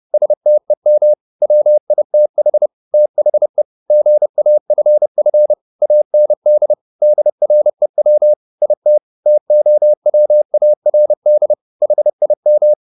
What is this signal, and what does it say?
Morse code, heard as a keyed tone, about 30 wpm, 600 Hz. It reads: STEM WITH THE GAFF AND DREW IT TOWARD HIM
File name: the old man and the sea / the old man and the sea 1 30wpm